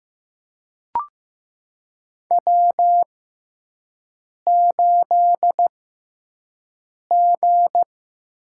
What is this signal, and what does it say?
Morse code, heard as a keyed tone, about 15 wpm, spaced as for 5 wpm, 700 Hz. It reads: W8G
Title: USA Call Signs - 5 Characters - 6x Character Spacing - 15wpm